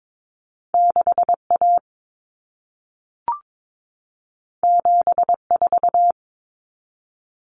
Morse code, heard as a keyed tone, about 22 wpm, 700 Hz. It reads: 6A 74
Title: Common Cut Numbers - 2-Digit 22wpm